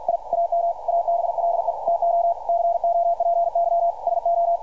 Morse code, heard as a keyed tone, about 52 wpm, 680 Hz. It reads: I AM ABLE TO COPY SO
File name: archived / CRSnet